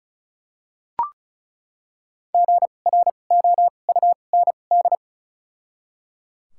Morse code, heard as a keyed tone, about 34 wpm, spaced as for 25 wpm, 700 Hz. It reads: GROUND
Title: 2x Character Spacing - Top 500 Words - 35wpm